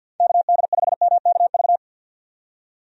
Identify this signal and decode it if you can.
Morse code, heard as a keyed tone, about 50 wpm, 700 Hz. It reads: KD5MKV
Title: USA Call Signs - 6 Characters - 50wpm